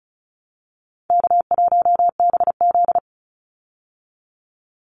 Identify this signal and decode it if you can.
Morse code, heard as a keyed tone, about 34 wpm, 700 Hz. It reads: K1BZ
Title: Call Signs Repeated 3x - 35wpm